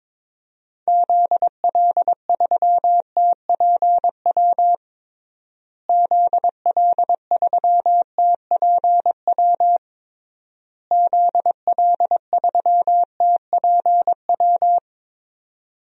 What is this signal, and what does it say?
Morse code, heard as a keyed tone, about 22 wpm, 700 Hz. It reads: ZL3TPW ZL3TPW ZL3TPW
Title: DXCC-100 Call Signs - 6 Characters - Repeated 3x - 22wpm